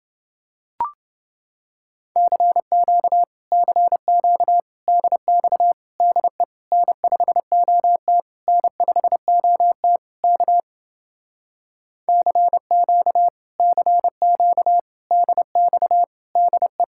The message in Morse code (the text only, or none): CQ CQ DX DE N5OT N5OT K CQ CQ DX DE